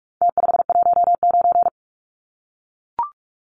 Morse code, heard as a keyed tone, about 44 wpm, 700 Hz. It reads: T519
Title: Numbers - Contest Style - 4-Digit 45wpm